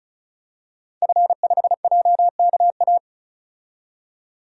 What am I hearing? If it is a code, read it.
Morse code, heard as a keyed tone, about 36 wpm, 700 Hz. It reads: F5JKA
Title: DXCC-100 Call Signs - 5 Characters - 35wpm